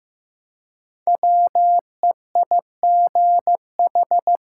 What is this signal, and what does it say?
Morse code, heard as a keyed tone, about 15 wpm, 700 Hz. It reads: WEIGH
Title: Five Letter Words - 15wpm